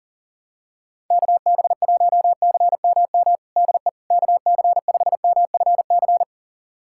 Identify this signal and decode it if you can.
Morse code, heard as a keyed tone, about 40 wpm, 700 Hz. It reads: KB1CMM DE KC5MFC
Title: USA Callsign DE Callsign - 6 Characters - 40wpm